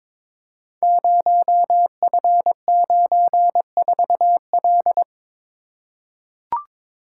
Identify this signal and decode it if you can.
Morse code, heard as a keyed tone, about 22 wpm, 700 Hz. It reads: 0F94L
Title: Code Groups - 5 Letter:Numbers - 22wpm